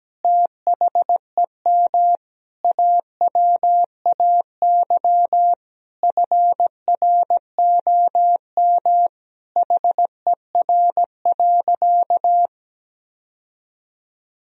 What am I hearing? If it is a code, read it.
Morse code, heard as a keyed tone, about 17 wpm, 700 Hz. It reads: THEM AWAY FROM HER.
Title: Sentences from Top 1000 Words - Review Part 2 - 17wpm